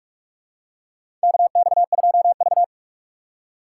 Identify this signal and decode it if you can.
Morse code, heard as a keyed tone, about 44 wpm, 690 Hz. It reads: KX2V
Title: USA Call Signs - 4 Characters - 2x Character Spacing - 45wpm